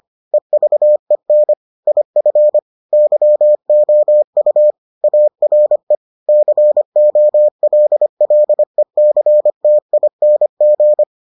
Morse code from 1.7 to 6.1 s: IF YOU ARE

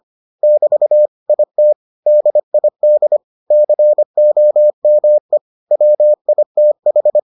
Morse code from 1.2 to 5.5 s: IT DID COME

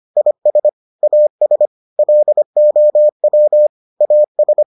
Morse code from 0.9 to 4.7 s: AS LOW AS

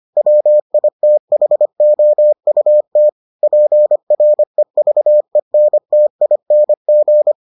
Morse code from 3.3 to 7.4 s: PREVENTING

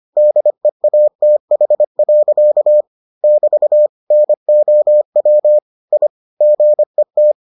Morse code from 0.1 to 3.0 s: DEATH.

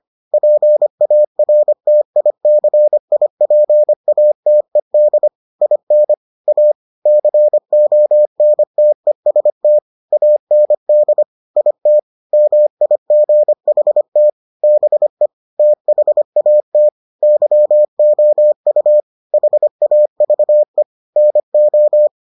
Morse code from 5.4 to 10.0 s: IN A CONTEST

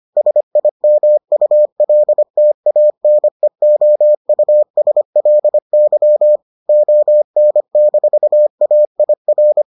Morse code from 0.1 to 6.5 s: SIMULTANEOUSLY

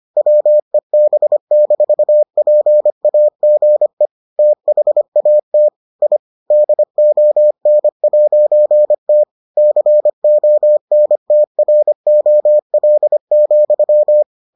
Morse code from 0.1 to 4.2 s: WEB-PAGE

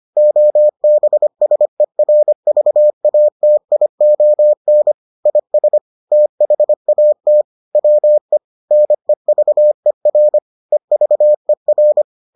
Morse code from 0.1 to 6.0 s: OBSERVATION IS

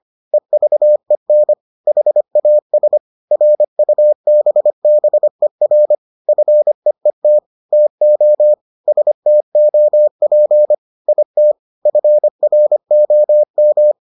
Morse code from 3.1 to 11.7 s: RUBBER FEET TO STOP IT

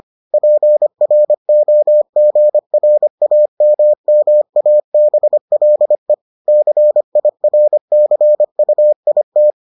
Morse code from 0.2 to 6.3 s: PROGRAMMABLE

